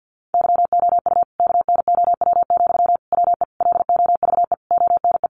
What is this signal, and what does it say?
Morse code, heard as a keyed tone, about 50 wpm, 700 Hz. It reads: YOU KNOW, WE LOVE ONE
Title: Sentences from Top 600 Words 50wpm